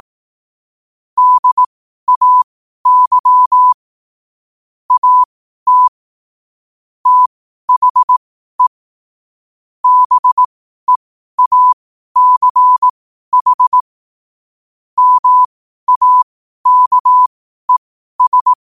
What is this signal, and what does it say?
Morse code, heard as a keyed tone, about 18 wpm, spaced as for 12 wpm, 1000 Hz. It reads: DAY AT THE BEACH MAKES
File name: qotdfiles / 12wpm